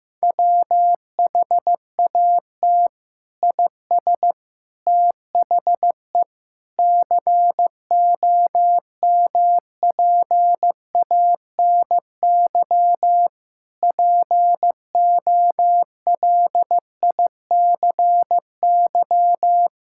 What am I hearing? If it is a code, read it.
Morse code, heard as a keyed tone, about 15 wpm, 700 Hz. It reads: WHAT IS THE COMPANY POLICY
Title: Sentences from Top 1000 Words - Review Part 5 - 15wpm